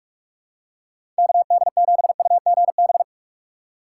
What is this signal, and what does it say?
Morse code, heard as a keyed tone, about 46 wpm, 700 Hz. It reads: KD7UGB
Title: USA Call Signs - 6 Characters - 6x Character Spacing - 45wpm